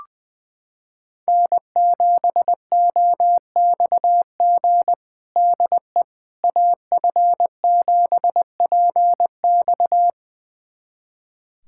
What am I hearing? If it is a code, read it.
Morse code, heard as a keyed tone, about 20 wpm, 700 Hz. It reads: N7OXG DE AF7PX
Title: USA Callsign DE Callsign - 5 Characters - 20wpm